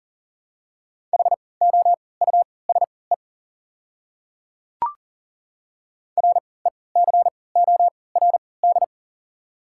Morse code, heard as a keyed tone, about 40 wpm, spaced as for 23 wpm, 690 Hz. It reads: HOUSE RECORD
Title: 3x Character Spacing - Top 500 Words - 40wpm